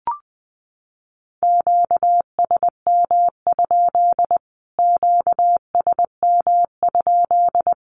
Morse code, heard as a keyed tone, about 20 wpm, 700 Hz. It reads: QSM? QSM?
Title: Q-Codes Repeated 3x 20wpm